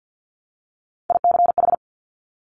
Morse code, heard as a keyed tone, about 50 wpm, 700 Hz. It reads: ICH
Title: Top Three Letter Combinations in Words 50wpm